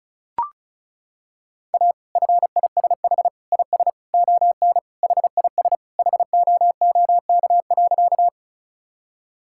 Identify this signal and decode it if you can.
Morse code, heard as a keyed tone, about 36 wpm, 700 Hz. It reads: A FISH IS ON HIS HOOK.
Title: Sentences from Top 1600 Words 35wpm